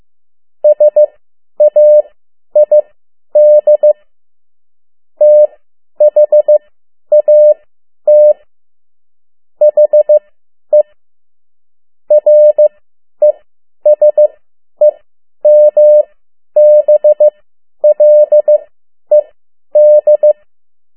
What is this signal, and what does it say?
Morse code, heard as a keyed tone, about 15 wpm, spaced as for 10 wpm, 600 Hz. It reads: SAID THAT HE RESEMBLED
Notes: Morse code 10 WPM